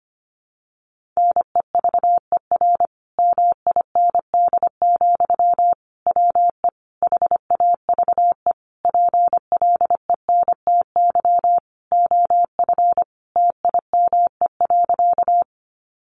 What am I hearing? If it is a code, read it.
Morse code, heard as a keyed tone, about 25 wpm, 700 Hz. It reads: NEVER MIND, WE HAVE PLENTY OF TIME.
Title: Sentences from Top 1300 Words 25wpm